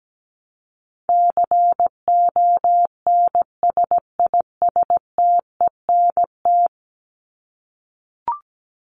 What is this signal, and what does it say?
Morse code, heard as a keyed tone, about 17 wpm, 700 Hz. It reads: CONSISTENT
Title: Top 1251-1500 Words 17wpm